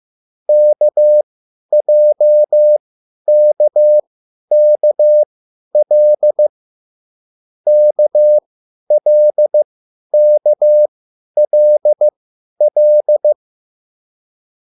Morse code, heard as a keyed tone, about 15 wpm, spaced as for 10 wpm, 600 Hz. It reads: KJKKL KLKLL